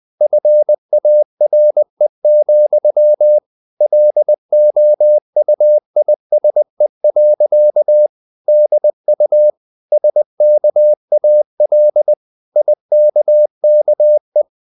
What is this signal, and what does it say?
Morse code, heard as a keyed tone, about 20 wpm, 600 Hz. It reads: FARE, LOUISE. DU SKAL IKKE
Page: Ferie 20 wpm